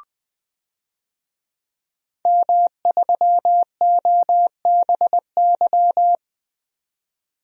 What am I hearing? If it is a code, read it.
Morse code, heard as a keyed tone, about 20 wpm, 700 Hz. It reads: M3OBY
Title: DXCC-100 Call Signs - 5 Characters - 20wpm